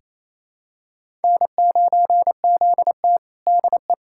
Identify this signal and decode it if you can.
Morse code, heard as a keyed tone, about 28 wpm, 700 Hz. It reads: N9ZT DE
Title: USA Callsign DE Callsign - 4 Characters - 28wpm